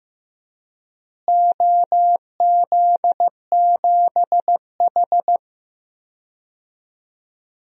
Morse code, heard as a keyed tone, about 14 wpm, 700 Hz. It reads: OZ7H
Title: DXCC-100 Call Signs - 4 Characters - Repeated 2x - 15wpm